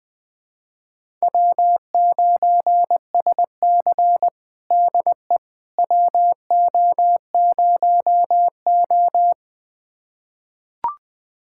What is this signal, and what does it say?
Morse code, heard as a keyed tone, about 21 wpm, 700 Hz. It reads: W9SC DE WO0O
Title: USA Callsign DE Callsign - 4 Characters - 20wpm